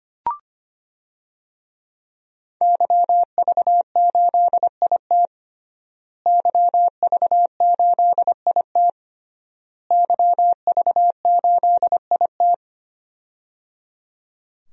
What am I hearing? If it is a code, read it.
Morse code, heard as a keyed tone, about 25 wpm, 700 Hz. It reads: YV8IT YV8IT YV8IT
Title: DXCC-100 Call Signs - 5 Characters - Repeated 3x - 25wpm